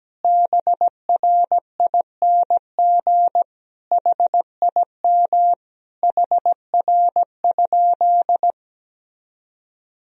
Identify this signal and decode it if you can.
Morse code, heard as a keyed tone, about 17 wpm, 700 Hz. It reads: BRING HIM HR?
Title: Sentences from Top 500 Words and Abbreviations 17wpm